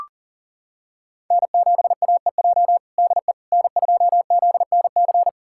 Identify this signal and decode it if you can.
Morse code, heard as a keyed tone, about 40 wpm, 700 Hz. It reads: N7AEJ DE N2ZNC
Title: USA Callsign DE Callsign 40wpm